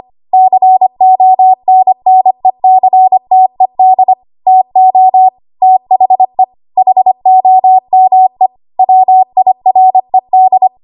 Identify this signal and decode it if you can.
Morse code, heard as a keyed tone, about 25 wpm, 750 Hz. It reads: CONNECTED TO THE HOME WIRED
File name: Archive / 25 WPM / 25WPM